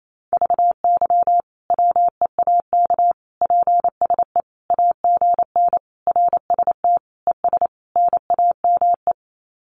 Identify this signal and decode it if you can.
Morse code, heard as a keyed tone, about 28 wpm, 700 Hz. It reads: VY WEAK PSE AGN RST ES NAME
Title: Common QSO Phrases 28wpm